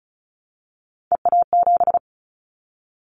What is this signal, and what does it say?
Morse code, heard as a keyed tone, about 34 wpm, 700 Hz. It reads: EA7
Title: Callsign Prefixes for Top 25 DXCC - Repeated 3x - 35wpm